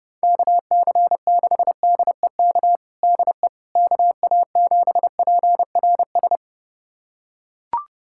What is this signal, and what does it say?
Morse code, heard as a keyed tone, about 32 wpm, 700 Hz. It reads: KC6DEK DE KA7PRS
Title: USA Callsign DE Callsign - 6 Characters - 30wpm